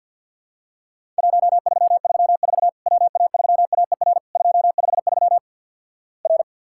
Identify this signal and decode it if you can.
Morse code, heard as a keyed tone, about 50 wpm, 700 Hz. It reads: 1334 WA3AER 253 R
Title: Worked All Europe DX Contest - QTC Exchange with Repeats - 50wpm